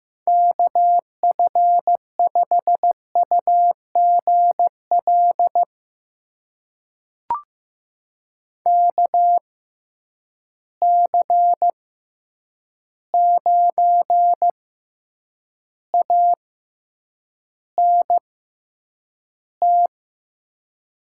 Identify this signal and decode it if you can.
Morse code, heard as a keyed tone, about 15 wpm, spaced as for 8 wpm, 700 Hz. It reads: KF5UGL KC9ANT
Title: USA Call Signs - 6 Characters - 6x Character Spacing - 15wpm